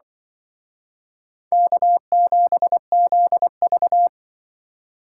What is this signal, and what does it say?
Morse code, heard as a keyed tone, about 24 wpm, 700 Hz. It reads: K7ZV
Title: Speed Racing - USA 1x2 Call Signs 20wpm